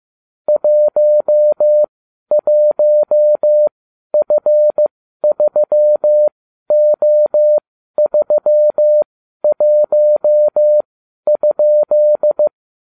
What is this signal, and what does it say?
Morse code, heard as a keyed tone, about 15 wpm, spaced as for 12 wpm, 600 Hz. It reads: 11F3O31?